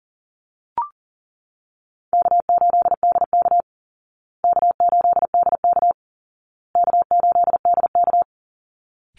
Morse code, heard as a keyed tone, about 40 wpm, 700 Hz. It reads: K8DK K8DK K8DK
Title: Easy Call Signs Repeated 3x - 40wpm